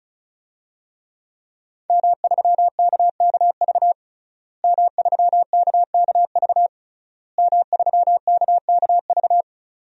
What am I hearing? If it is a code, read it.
Morse code, heard as a keyed tone, about 36 wpm, 700 Hz. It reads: M3KKV M3KKV M3KKV
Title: DXCC-100 Call Signs - 5 Characters - Repeated 3x - 35wpm